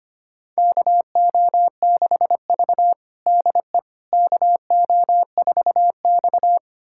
Morse code, heard as a keyed tone, about 25 wpm, 700 Hz. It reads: KO6V DE KO4X
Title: USA Callsign DE Callsign - 4 Characters - 25wpm